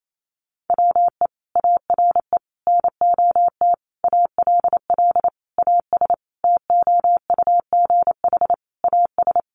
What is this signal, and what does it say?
Morse code, heard as a keyed tone, about 28 wpm, 700 Hz. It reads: WE ARE NOT ALL AS TOUGH AS
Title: Sentences from Top 1200 Words 28wpm